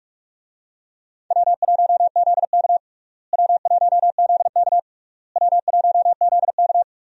Morse code, heard as a keyed tone, about 42 wpm, 700 Hz. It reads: W1ZK W1ZK W1ZK
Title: USA Call Signs - 4 Characters - Repeated 3x - 45wpm